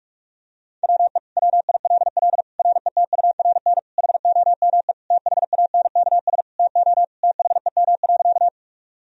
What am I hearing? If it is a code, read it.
Morse code, heard as a keyed tone, about 46 wpm, 700 Hz. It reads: WE WILL RETURN HOME THANKS TO THEM.
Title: Sentences from Top 1000 Words - Review Part 3 - 45wpm